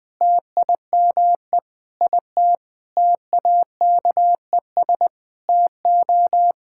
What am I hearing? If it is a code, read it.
Morse code, heard as a keyed tone, about 20 wpm, 700 Hz. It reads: TIME IT TAKES TO
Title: Sentences with 7-Word Phrases - 20wpm